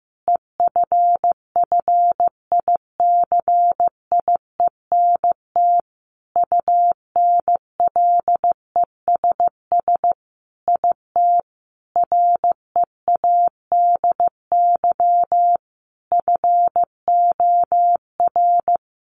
Morse code, heard as a keyed tone, about 15 wpm, 700 Hz. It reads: EFFICIENT UNLESS IT READY FOR
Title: Sentences from Top 1300 Words 15wpm